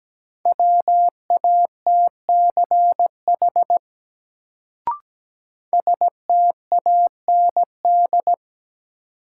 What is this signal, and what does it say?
Morse code, heard as a keyed tone, about 17 wpm, 700 Hz. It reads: WATCH STAND
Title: Sentences from Top 300 Words - New Word Drill - 17wpm